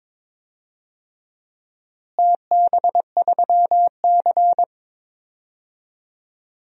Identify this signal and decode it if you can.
Morse code, heard as a keyed tone, about 22 wpm, 700 Hz. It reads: TB3C
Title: DXCC-100 Call Signs - 4 Characters - Repeated 2x - 22wpm